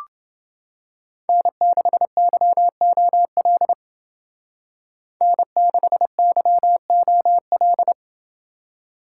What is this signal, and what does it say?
Morse code, heard as a keyed tone, about 29 wpm, 700 Hz. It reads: N6YOL N6YOL